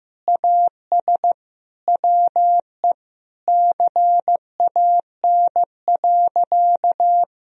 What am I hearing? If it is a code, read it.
Morse code, heard as a keyed tone, about 15 wpm, 700 Hz. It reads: AS WE CAN.
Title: Sentences from Top 1000 Words - Review Part 3 - 15wpm